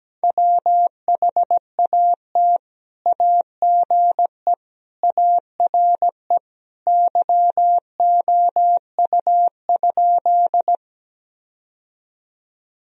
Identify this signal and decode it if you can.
Morse code, heard as a keyed tone, about 17 wpm, 700 Hz. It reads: WHAT AGE ARE YOU?
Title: Sentences from Top 1000 Words - Review Part 3 - 17wpm